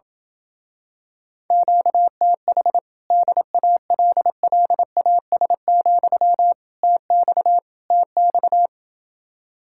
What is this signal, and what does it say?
Morse code, heard as a keyed tone, about 28 wpm, 700 Hz. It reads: QTH DALLAS, TX TX